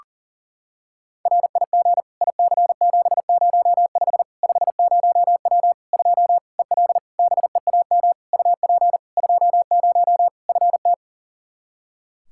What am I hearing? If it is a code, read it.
Morse code, heard as a keyed tone, about 38 wpm, 700 Hz. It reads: RIG IC705 50W 2 EL BEAM UP 20 FT
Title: Practice QSO - Exchange 2 - Info - 40wpm